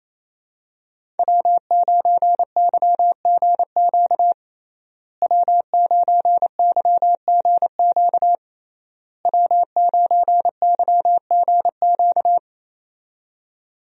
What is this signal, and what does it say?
Morse code, heard as a keyed tone, about 28 wpm, 700 Hz. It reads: W9YGQ W9YGQ W9YGQ
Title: USA Call Signs - 5 Characters - Repeated 3x - 28wpm